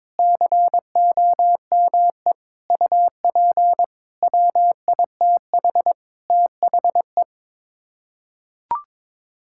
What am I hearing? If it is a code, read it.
Morse code, heard as a keyed tone, about 22 wpm, 700 Hz. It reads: COME UP WITH THE
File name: practice / 22wpm / Sets of 7 Words - 22wpm